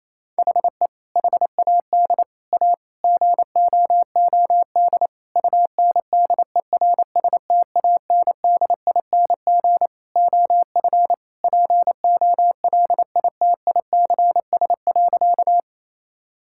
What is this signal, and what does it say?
Morse code, heard as a keyed tone, about 28 wpm, 700 Hz. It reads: HE HAD A GOOD UNDERSTANDING OF POLITICS.
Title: Sentences from Top 1000 Words - Review Part 6 - 28wpm